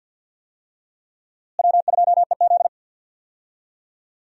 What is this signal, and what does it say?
Morse code, heard as a keyed tone, about 52 wpm, 700 Hz. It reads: W2EZ
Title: USA Call Signs - 4 Characters - Repeated 3x - 50wpm